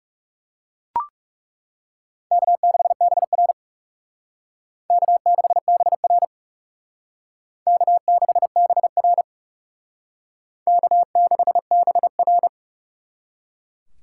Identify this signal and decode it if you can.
Morse code, heard as a keyed tone, about 36 wpm, 700 Hz. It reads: K6BR K6BR K6BR K6BR
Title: Speed Racing - USA 1x2 Call Signs 30wpm